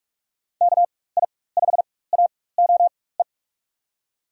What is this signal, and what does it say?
Morse code, heard as a keyed tone, about 43 wpm, spaced as for 21 wpm, 700 Hz. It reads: KI5AOE
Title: USA Call Signs - 4x Character Spacing - 45wpm